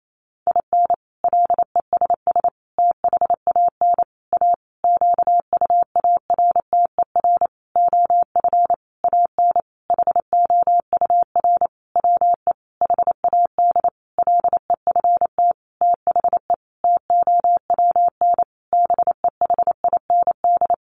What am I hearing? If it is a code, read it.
Morse code, heard as a keyed tone, about 28 wpm, 700 Hz. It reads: IN LESS THAN A QUARTER OF AN HOUR WE HAD LEFT THE TOWN BEHIND
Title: Sentences from Top 1300 Words 28wpm